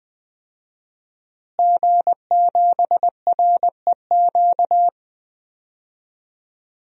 Morse code, heard as a keyed tone, about 20 wpm, 700 Hz. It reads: G7REQ
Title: DXCC-100 Call Signs - 5 Characters - Repeated 3x - 20wpm